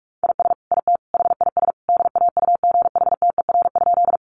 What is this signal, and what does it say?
Morse code, heard as a keyed tone, about 46 wpm, 700 Hz. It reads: IS IT HIS DAUGHTER?
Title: Sentences from Top 1500 Words 45wpm